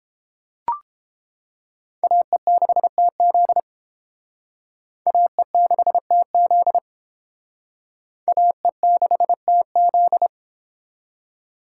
Morse code, heard as a keyed tone, about 29 wpm, 700 Hz. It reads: AE6TZ AE6TZ AE6TZ